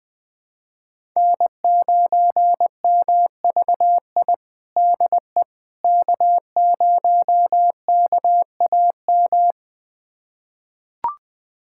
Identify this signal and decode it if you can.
Morse code, heard as a keyed tone, about 20 wpm, 700 Hz. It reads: N9MVI DE K0KAM
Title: USA Callsign DE Callsign - 5 Characters - 20wpm